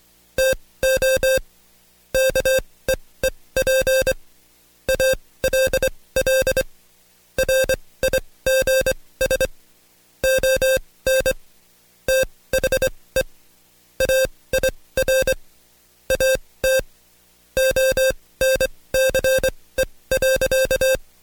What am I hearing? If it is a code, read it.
Morse code, heard as a keyed tone, about 24 wpm, spaced as for 18 wpm, 530 Hz. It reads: TO KEEP ALL RIGS ON THE AIR AT ONCE.
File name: Code Quick / AudioDownld2 / Seg135-Stepper8 17WPM